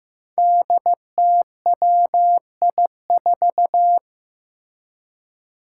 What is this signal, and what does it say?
Morse code, heard as a keyed tone, about 15 wpm, 700 Hz. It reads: DTWI4
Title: Five Challenging Letter-Number Combinations 15wpm